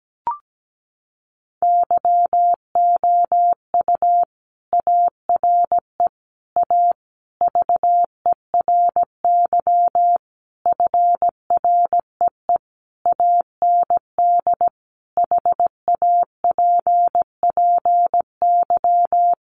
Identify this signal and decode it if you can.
Morse code, heard as a keyed tone, about 17 wpm, 700 Hz. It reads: YOU ARE A VERY FREE AND HAPPY
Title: Sentences from Top 1000 Words 17wpm